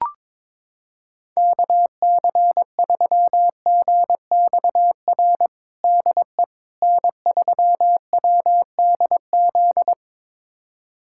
Morse code, heard as a keyed tone, about 21 wpm, 700 Hz. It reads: KC3GXR DE N3WDZ
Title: USA Callsign DE Callsign 22wpm